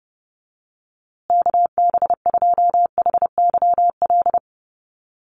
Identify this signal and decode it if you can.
Morse code, heard as a keyed tone, about 30 wpm, 700 Hz. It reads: KB2HYL
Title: USA Call Signs Repeated 2x 30wpm